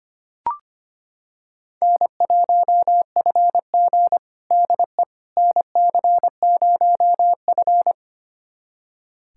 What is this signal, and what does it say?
Morse code, heard as a keyed tone, about 25 wpm, 700 Hz. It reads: N1FG DE NC0F
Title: USA Callsign DE Callsign - 4 Characters - 25wpm